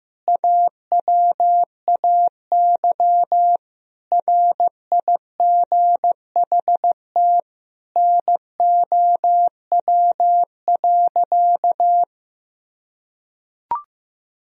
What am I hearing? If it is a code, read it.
Morse code, heard as a keyed tone, about 15 wpm, 700 Hz. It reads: AWAY RIGHT NOW.
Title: Sentences from Top 1000 Words - Review Part 4 - 15wpm